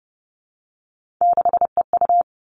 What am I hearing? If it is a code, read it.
Morse code, heard as a keyed tone, about 30 wpm, 700 Hz. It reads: 6EU
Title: All Cut Numbers - 3-Digit 30wpm